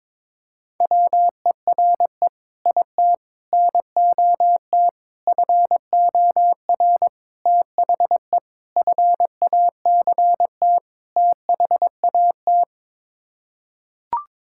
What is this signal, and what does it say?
Morse code, heard as a keyed tone, about 22 wpm, 700 Hz. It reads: WERE IT NOT FOR THE FACT THAT
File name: practice / 22wpm / Sets of 7 Words - 22wpm